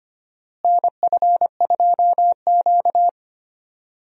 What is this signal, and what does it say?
Morse code, heard as a keyed tone, about 25 wpm, 700 Hz. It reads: NF2Q
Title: USA Call Signs - 4 Characters - Repeated 3x - 25wpm